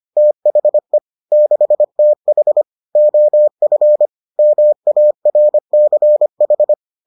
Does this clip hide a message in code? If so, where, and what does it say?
Yes: Morse code from 0.2 to 6.7 s, heard as a keyed tone, about 25 wpm, 600 Hz. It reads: THE 6TH OF MARCH